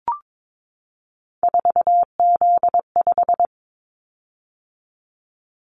Morse code, heard as a keyed tone, about 22 wpm, 700 Hz. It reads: 4Z5
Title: Callsign Prefixes for Top 50 DXCC - 22wpm